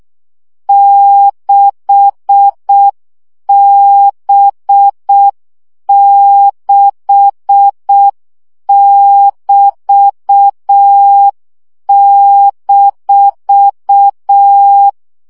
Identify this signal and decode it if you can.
Morse code, heard as a keyed tone, about 6 wpm, 800 Hz. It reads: 6B6<BT>-
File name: CW gyakorlas b 6 egyenlo minusz 5 baud